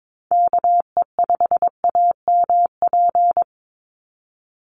Morse code, heard as a keyed tone, about 22 wpm, 700 Hz. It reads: KE5AMP
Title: USA Call Signs 22wpm